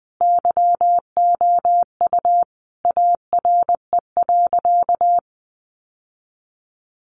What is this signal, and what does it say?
Morse code, heard as a keyed tone, about 19 wpm, 700 Hz. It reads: YOU ARE.
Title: Sentences from Top 1200 Words 20wpm